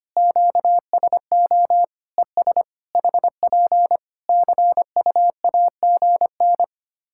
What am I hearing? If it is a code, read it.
Morse code, heard as a keyed tone, about 25 wpm, 700 Hz. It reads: QSO ES HP CUAGN
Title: Practice QSO - Exchange 4 - Ending - 25wpm